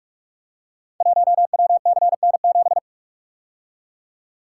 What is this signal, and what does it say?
Morse code, heard as a keyed tone, about 44 wpm, 690 Hz. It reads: 1WCN7
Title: Code Groups - 5 Letter:Numbers - 45wpm